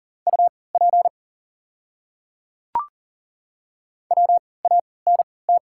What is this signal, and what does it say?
Morse code, heard as a keyed tone, about 40 wpm, spaced as for 23 wpm, 700 Hz. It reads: UP WANT
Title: 3x Character Spacing - Top 200 Words - 40wpm